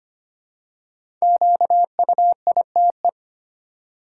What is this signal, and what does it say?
Morse code, heard as a keyed tone, about 25 wpm, 700 Hz. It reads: QUITE
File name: 5 Letter Words - First 3 Letters Given - 25wpm